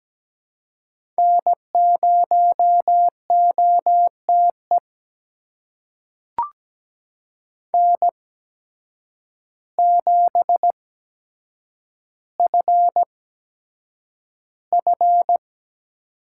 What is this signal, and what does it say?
Morse code, heard as a keyed tone, about 17 wpm, spaced as for 8 wpm, 700 Hz. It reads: N0OTE N7FF
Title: USA Call Signs - 8x Character Spacing - 17wpm